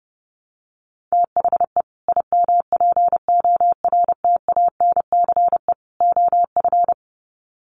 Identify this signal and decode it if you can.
Morse code, heard as a keyed tone, about 29 wpm, 700 Hz. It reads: THE IMPORTANCE OF
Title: Sets of 3 Words 30wpm